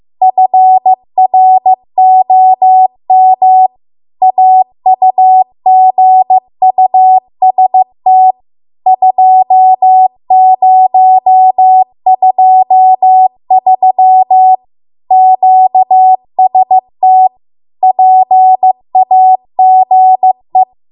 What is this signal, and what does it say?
Morse code, heard as a keyed tone, about 15 wpm, 750 Hz. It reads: FROM AUGUST 2023 QST PAGE